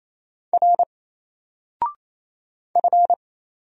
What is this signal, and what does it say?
Morse code, heard as a keyed tone, about 28 wpm, 700 Hz. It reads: R F
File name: Single-Letter-Number-28wpm